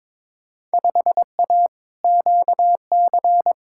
Morse code, heard as a keyed tone, about 22 wpm, 700 Hz. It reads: 5A QC
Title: Field Day Exchanges Encore - 22wpm